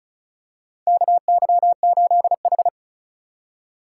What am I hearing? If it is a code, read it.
Morse code, heard as a keyed tone, about 36 wpm, 700 Hz. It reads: KY8H
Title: USA Call Signs - 4 Characters - 4x Character Spacing - 35wpm